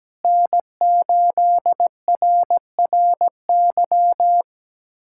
Morse code, heard as a keyed tone, about 17 wpm, 700 Hz. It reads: N8RRY
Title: USA Callsign DE Callsign - 5 Characters - 17wpm